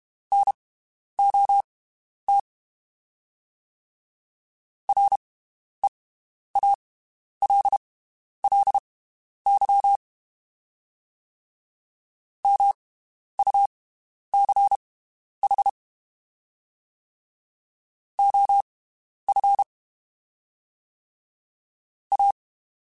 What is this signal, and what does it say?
Morse code, heard as a keyed tone, about 32 wpm, spaced as for 11 wpm, 800 Hz. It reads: NOT REALLY MUCH OF A
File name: (spaced) / QSOs Spaced-mp3s / WD0BGP 1 @30 sp